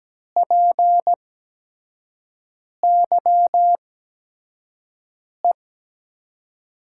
Morse code, heard as a keyed tone, about 17 wpm, spaced as for 5 wpm, 700 Hz. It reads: PYE
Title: USA Call Signs - 8x Character Spacing - 17wpm